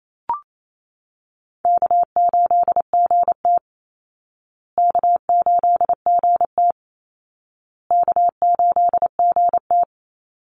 Morse code, heard as a keyed tone, about 28 wpm, 700 Hz. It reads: K8GT K8GT K8GT
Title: Call Signs Repeated 2x - 28wpm